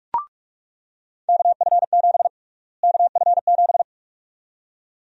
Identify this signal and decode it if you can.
Morse code, heard as a keyed tone, about 46 wpm, 700 Hz. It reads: KF7 KF7
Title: USA Call Sign Prefixes - Repeated 2x - 45wpm